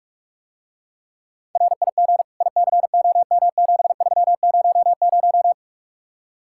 Morse code, heard as a keyed tone, about 50 wpm, 700 Hz. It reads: RIG ICOM7300